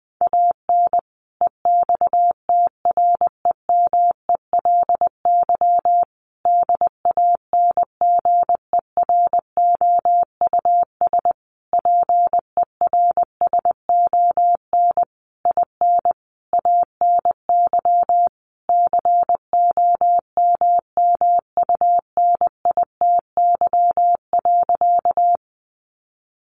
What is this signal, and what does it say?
Morse code, heard as a keyed tone, about 20 wpm, 700 Hz. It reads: AN EXTREMELY DANGEROUS PERSON IN ANY COMMUNITY.
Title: Sentences from Top 1200 Words 20wpm